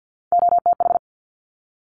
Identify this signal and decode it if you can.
Morse code, heard as a keyed tone, about 50 wpm, 700 Hz. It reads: OTH
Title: Top Three Letter Combinations in Words 50wpm